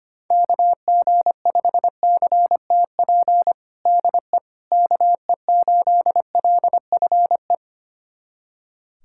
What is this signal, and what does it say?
Morse code, heard as a keyed tone, about 25 wpm, 700 Hz. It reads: KG5CTP DE KE8LFE
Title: USA Callsign DE Callsign - 6 Characters - 25wpm